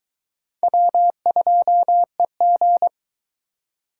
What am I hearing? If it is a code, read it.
Morse code, heard as a keyed tone, about 22 wpm, 700 Hz. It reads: W2EG